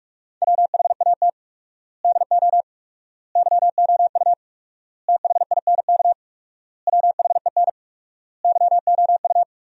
Morse code, heard as a keyed tone, about 41 wpm, 700 Hz. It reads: WHAT DO YOU THINK WHEN YOU
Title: 4x Word-Spacing - Sentences from Top 300 Words - 45wpm